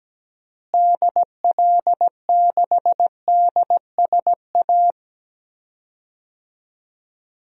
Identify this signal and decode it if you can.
Morse code, heard as a keyed tone, about 17 wpm, 700 Hz. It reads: DL6DSA
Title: DXCC-100 Call Signs - 6 Characters - Repeated 2x - 17wpm